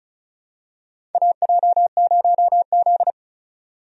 Morse code, heard as a keyed tone, about 36 wpm, 700 Hz. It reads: AJ0Z
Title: USA Call Signs - 4 Characters - 35wpm